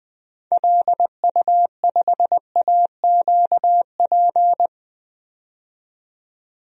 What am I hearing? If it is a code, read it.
Morse code, heard as a keyed tone, about 20 wpm, 700 Hz. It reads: LU5AQP
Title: DXCC-100 Call Signs - 6 Characters - 20wpm